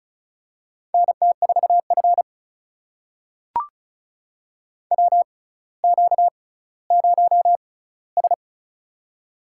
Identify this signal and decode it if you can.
Morse code, heard as a keyed tone, about 36 wpm, spaced as for 18 wpm, 700 Hz. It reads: NT4F WQ0S
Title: USA Call Signs - 4 Characters - 6x Character Spacing - 35wpm